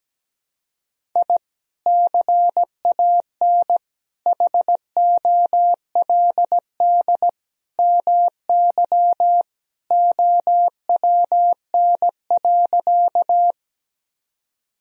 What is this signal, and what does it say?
Morse code, heard as a keyed tone, about 17 wpm, 700 Hz. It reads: I CAN HOLD MY OWN.
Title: Sentences from Top 1000 Words - Review Part 2 - 17wpm